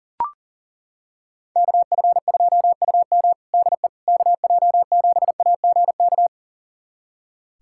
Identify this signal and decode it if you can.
Morse code, heard as a keyed tone, about 40 wpm, 700 Hz. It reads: KF2UM DE KJ7AGK
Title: USA Callsign DE Callsign 40wpm